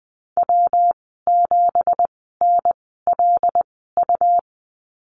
Morse code, heard as a keyed tone, about 20 wpm, spaced as for 14 wpm, 700 Hz. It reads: W7NLU
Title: 193 Lesson 31g - Letter v - Call Signs 20wpm